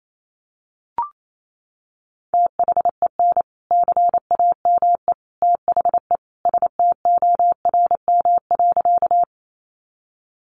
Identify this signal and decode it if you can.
Morse code, heard as a keyed tone, about 28 wpm, 700 Hz. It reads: THEN CAME THE STORM.
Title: Sentences from Top 1500 Words 28wpm